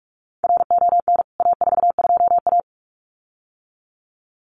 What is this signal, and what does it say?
Morse code, heard as a keyed tone, about 44 wpm, 700 Hz. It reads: RON A42A
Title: CWT Exchange 45wpm